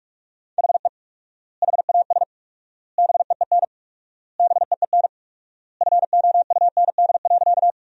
Morse code, heard as a keyed tone, about 47 wpm, 700 Hz. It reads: HE HAS BEEN BEEN FOUND.
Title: 4x Word-Spacing - Sentences from Top 300 Words - 45wpm